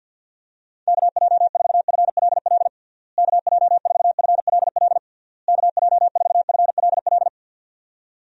Morse code, heard as a keyed tone, about 51 wpm, 700 Hz. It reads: KJ4FLL KJ4FLL KJ4FLL
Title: USA Call Signs - 6 Characters - Repeated 3x - 50wpm